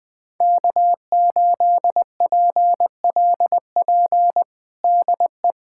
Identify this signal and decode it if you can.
Morse code, heard as a keyed tone, about 21 wpm, 700 Hz. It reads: K8PLP DE
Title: USA Callsign DE Callsign - 5 Characters - 20wpm